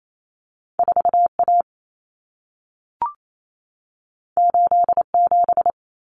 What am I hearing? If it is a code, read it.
Morse code, heard as a keyed tone, about 28 wpm, 700 Hz. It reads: 4A 87
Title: Common Cut Numbers - 2-Digit 28wpm